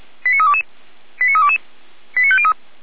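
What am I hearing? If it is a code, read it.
Five-tone selective call (ZVEI), heard as quick runs of five tones, tones of about 70 ms. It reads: 89322 89322 89503
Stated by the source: ZVEI five-tone call received by SDR, recorded by Feuerwehr Orlamünde.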